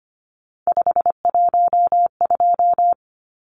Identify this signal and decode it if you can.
Morse code, heard as a keyed tone, about 25 wpm, 700 Hz. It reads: 512
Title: Numbers - 3-Digit 25wpm